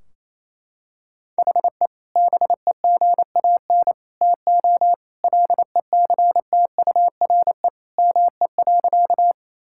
Morse code, heard as a keyed tone, about 28 wpm, 700 Hz. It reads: HE BEGAN TO LECTURE ME.
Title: Sentences from Top 1600 Words 28wpm